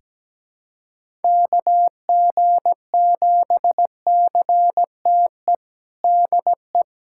Morse code, heard as a keyed tone, about 17 wpm, 700 Hz. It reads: KG7CTE DE
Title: USA Callsign DE Callsign - 6 Characters - 17wpm